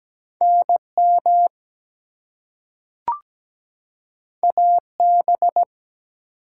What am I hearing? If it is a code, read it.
Morse code, heard as a keyed tone, about 17 wpm, 700 Hz. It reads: NM AB
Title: US States and Canadian Provinces 17wpm